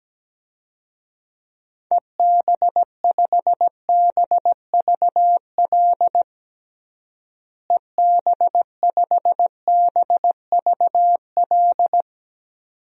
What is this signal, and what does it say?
Morse code, heard as a keyed tone, about 17 wpm, 700 Hz. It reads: EB5BVL EB5BVL
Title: DXCC-100 Call Signs - 6 Characters - Repeated 2x - 17wpm